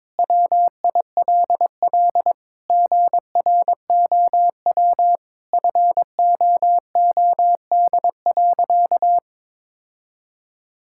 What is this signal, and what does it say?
Morse code, heard as a keyed tone, about 21 wpm, 700 Hz. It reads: WILL GROW FOOD.
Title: Sentences from Top 1000 Words - Review Part 6 - 22wpm